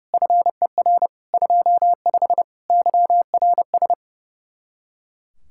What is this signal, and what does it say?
Morse code, heard as a keyed tone, about 36 wpm, 700 Hz. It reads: FER 25 YRS
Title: Speed-Racing Common QSO Phrases - 30wpm